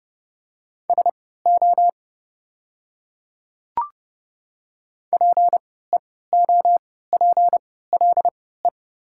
Morse code, heard as a keyed tone, about 30 wpm, spaced as for 17 wpm, 700 Hz. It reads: SO PEOPLE
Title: 3x Character Spacing - Top 100 Words - 30wpm